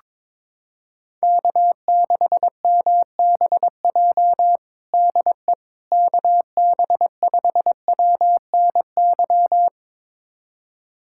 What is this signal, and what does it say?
Morse code, heard as a keyed tone, about 21 wpm, 700 Hz. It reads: K6MBJ DE KB5WNY
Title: USA Callsign DE Callsign 22wpm